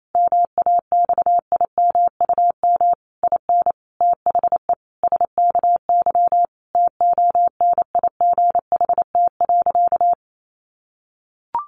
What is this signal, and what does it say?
Morse code, heard as a keyed tone, about 28 wpm, 700 Hz. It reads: MAXIMUM IN THE SKY TONIGHT.
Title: Sentences from Top 1300 Words 28wpm